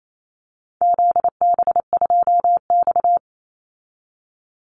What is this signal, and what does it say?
Morse code, heard as a keyed tone, about 28 wpm, 700 Hz. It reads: ZB2X
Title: Call Signs Repeated 2x - 28wpm